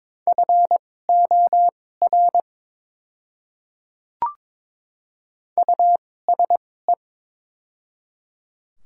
Morse code, heard as a keyed tone, about 21 wpm, spaced as for 16 wpm, 700 Hz. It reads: FOR USE
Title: 2x Character Spacing - Top 200 Words - 22wpm